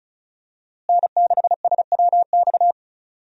Morse code, heard as a keyed tone, about 36 wpm, 700 Hz. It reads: N6SWX
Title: USA Call Signs - 5 Characters - Repeated 2x - 35wpm